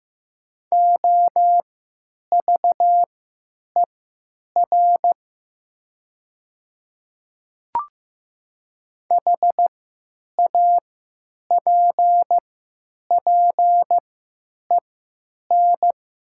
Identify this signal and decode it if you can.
Morse code, heard as a keyed tone, about 15 wpm, spaced as for 9 wpm, 700 Hz. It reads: OVER HAPPEN
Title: 3x Character Spacing - Top 500 Words - 15wpm